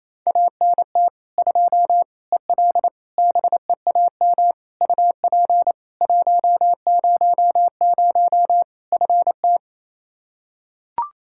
Morse code, heard as a keyed tone, about 28 wpm, 700 Hz. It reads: ANT 2 EL BEAM UP 100 FT
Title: K1USN SST QSO Practice 28wpm